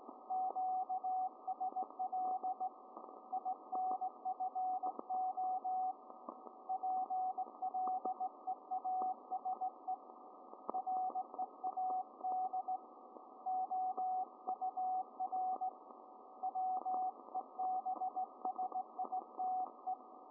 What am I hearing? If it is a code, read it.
Morse code, heard as a keyed tone, about 18 wpm, 750 Hz, with radio noise. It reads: QSL INFO PLEASE READ OUR WEBSITE